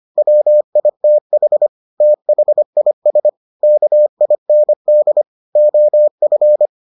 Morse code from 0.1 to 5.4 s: WITH THIS KIND

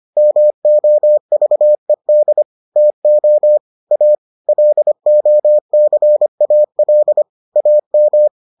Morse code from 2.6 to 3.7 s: TO

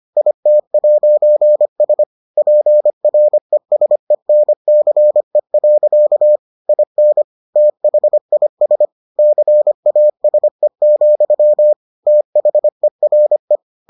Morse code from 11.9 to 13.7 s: THERE

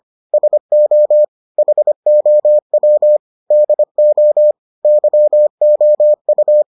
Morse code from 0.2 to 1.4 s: SO

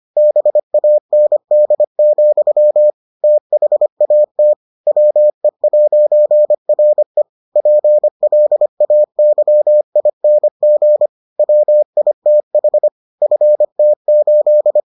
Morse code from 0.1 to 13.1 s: BAND, THAT WE'RE PLAYING WITH

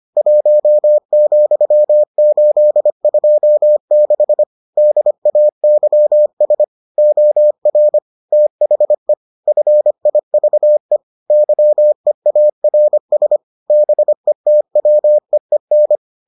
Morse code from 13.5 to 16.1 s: BETWEEN

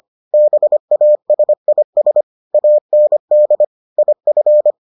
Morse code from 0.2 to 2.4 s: BASIS

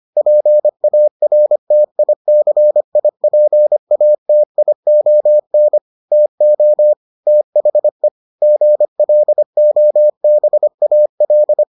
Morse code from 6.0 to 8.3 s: TO THE

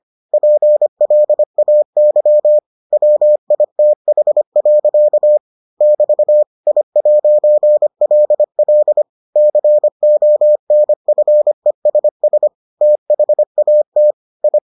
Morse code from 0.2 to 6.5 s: PLAY WITH. <BT>

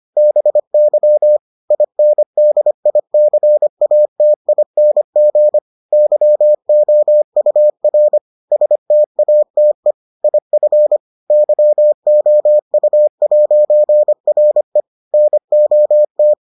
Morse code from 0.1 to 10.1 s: BY INDICATING YOUR STATE

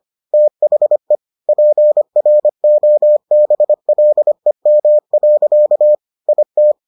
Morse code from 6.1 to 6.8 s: IT

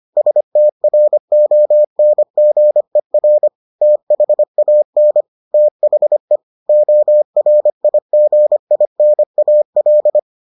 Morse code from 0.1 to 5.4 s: STRONGER THAN

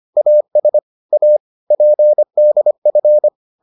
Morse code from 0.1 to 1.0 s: AS

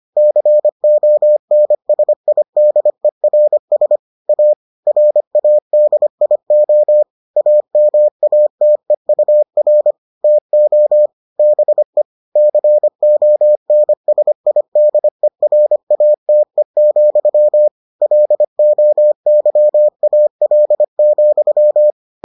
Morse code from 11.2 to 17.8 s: BE CONSIDERATE,

